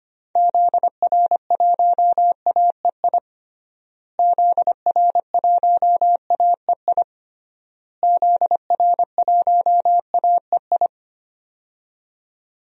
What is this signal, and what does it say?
Morse code, heard as a keyed tone, about 25 wpm, 700 Hz. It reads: ZR1AEI ZR1AEI ZR1AEI
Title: DXCC-100 Call Signs - 6 Characters - Repeated 3x - 25wpm